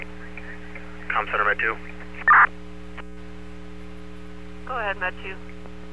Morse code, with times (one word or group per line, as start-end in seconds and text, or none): none